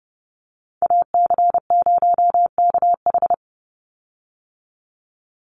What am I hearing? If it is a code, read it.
Morse code, heard as a keyed tone, about 30 wpm, 700 Hz. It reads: AC0KH
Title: USA Call Signs 30wpm